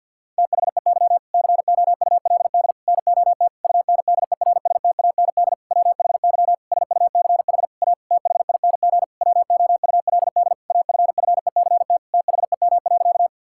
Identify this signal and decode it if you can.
Morse code, heard as a keyed tone, about 50 wpm, 700 Hz. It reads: THEY COULD NOT UNDERSTAND WHY SUCH A THING WOULD AFFECT THEM.
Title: Sentences from Top 1000 Words - Review Part 6 - 50wpm